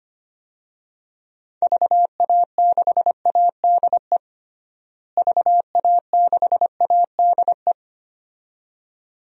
Morse code, heard as a keyed tone, about 25 wpm, 700 Hz. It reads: VA6ADE VA6ADE
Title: DXCC-100 Call Signs - 6 Characters - Repeated 2x - 25wpm